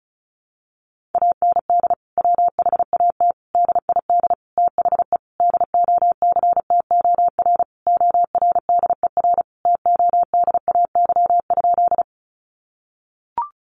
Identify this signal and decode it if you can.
Morse code, heard as a keyed tone, about 34 wpm, 700 Hz. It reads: AND WHAT DID THE DOCTOR ORDER TODAY?
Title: Sentences from Top 1500 Words 35wpm